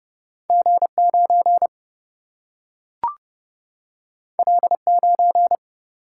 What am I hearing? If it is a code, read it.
Morse code, heard as a keyed tone, about 30 wpm, 700 Hz. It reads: G 9 L 9
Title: Single-Letter Single-Number 30wpm